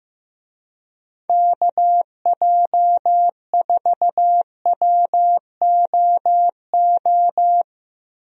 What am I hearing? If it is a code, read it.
Morse code, heard as a keyed tone, about 15 wpm, 700 Hz. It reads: KJ4WOO
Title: USA Call Signs - 6 Characters - Repeated 3x - 15wpm